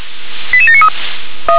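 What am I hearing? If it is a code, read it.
Five-tone selective call (ZVEI), heard as quick runs of five tones, tones of about 70 ms. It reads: 88793
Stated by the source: ZVEI five-tone call received by SDR.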